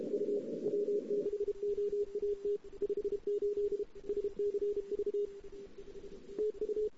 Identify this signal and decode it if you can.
Morse code, heard as a keyed tone, about 31 wpm, 410 Hz. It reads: U W6OAT 58 SCV TU
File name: CW / 2008-ss-cw-k5tr-op-n5rz -08:00